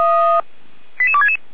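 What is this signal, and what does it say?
Five-tone selective call (ZVEI), heard as quick runs of five tones, tones of about 70 ms. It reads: 88266
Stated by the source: ZVEI five-tone call received by SDR.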